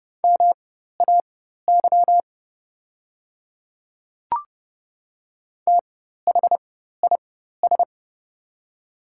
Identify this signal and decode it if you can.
Morse code, heard as a keyed tone, about 29 wpm, spaced as for 14 wpm, 700 Hz. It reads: MAY THIS
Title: 4x Character Spacing - Top 100 Words - 30wpm